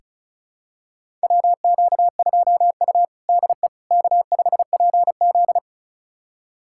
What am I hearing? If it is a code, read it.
Morse code, heard as a keyed tone, about 36 wpm, 700 Hz. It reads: WQ2U DE K5PZ
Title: USA Callsign DE Callsign - 4 Characters - 35wpm